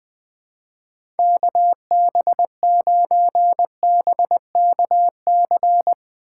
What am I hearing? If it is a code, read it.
Morse code, heard as a keyed tone, about 21 wpm, 700 Hz. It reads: KB9BKC
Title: USA Call Signs - 6 Characters - 4x Character Spacing - 20wpm